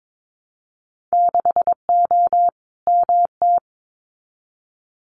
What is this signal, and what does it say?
Morse code, heard as a keyed tone, about 22 wpm, 700 Hz. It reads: 6O MT
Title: Winter Field Day Exchanges Encore 22wpm